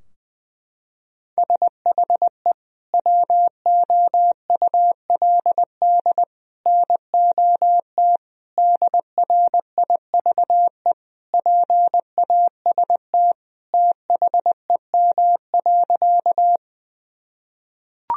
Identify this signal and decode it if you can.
Morse code, heard as a keyed tone, about 20 wpm, 700 Hz. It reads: SHE WOULD NOT DRIVE PAST THEM.
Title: Sentences from Top 1000 Words - Review Part 3 - 20wpm